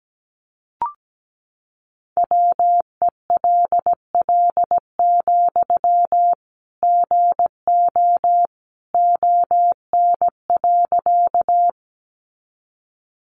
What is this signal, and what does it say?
Morse code, heard as a keyed tone, about 17 wpm, 700 Hz. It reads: WELL, GO ON.
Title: Sentences from Top 200 Words 17wpm